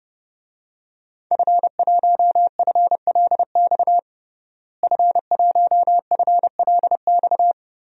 Morse code, heard as a keyed tone, about 32 wpm, 700 Hz. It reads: F1FLX F1FLX
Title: DXCC-100 Call Signs - 5 Characters - Repeated 3x - 30wpm